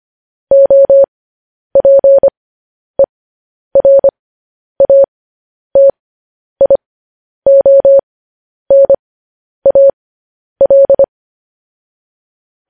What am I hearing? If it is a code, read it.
Morse code, heard as a keyed tone, about 25 wpm, spaced as for 10 wpm, 550 Hz. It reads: OPERATIONAL
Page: News Headlines in Morse Code at 10 WPM